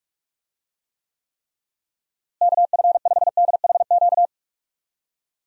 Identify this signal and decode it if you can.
Morse code, heard as a keyed tone, about 46 wpm, 700 Hz. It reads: KF5DHQ
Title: USA Call Signs - 6 Characters - Repeated 3x - 45wpm